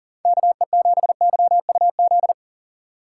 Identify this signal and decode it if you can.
Morse code, heard as a keyed tone, about 52 wpm, 700 Hz. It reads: KE7YUZ